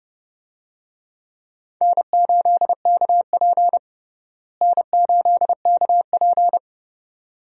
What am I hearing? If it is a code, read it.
Morse code, heard as a keyed tone, about 29 wpm, 700 Hz. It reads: N8KP N8KP
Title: USA Call Signs - 4 Characters - Repeated 2x - 30wpm